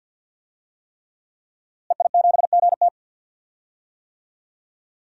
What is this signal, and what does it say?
Morse code, heard as a keyed tone, about 52 wpm, 700 Hz. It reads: EI7GT
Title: DXCC-100 Call Signs - 5 Characters - Repeated 3x - 50wpm